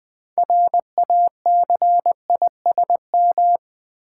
Morse code, heard as a keyed tone, about 20 wpm, 700 Hz. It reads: RACISM
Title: Build A Word - 2 Letters At A Time - 6-Letter Words - 20wpm